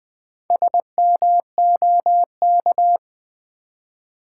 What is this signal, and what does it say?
Morse code, heard as a keyed tone, about 20 wpm, 700 Hz. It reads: SMOK
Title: Build A Word - 1 Letter At A Time - 3 to 5-Letter Words - 20wpm